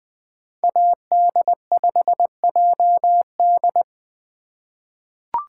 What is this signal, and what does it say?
Morse code, heard as a keyed tone, about 20 wpm, 700 Hz. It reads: AD5JD
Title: USA Call Signs - 4x Character Spacing - 20wpm